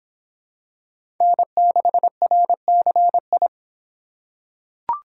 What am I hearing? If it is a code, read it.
Morse code, heard as a keyed tone, about 24 wpm, 700 Hz. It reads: N6RCI